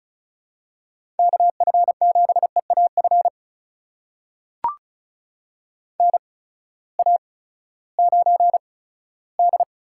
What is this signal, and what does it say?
Morse code, heard as a keyed tone, about 36 wpm, spaced as for 18 wpm, 700 Hz. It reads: KF7EAF NA9D
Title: USA Call Signs - 8x Character Spacing - 35wpm